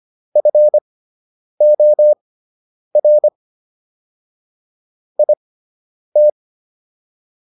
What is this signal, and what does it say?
Morse code, heard as a keyed tone, about 25 wpm, spaced as for 9 wpm, 600 Hz. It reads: FOR IT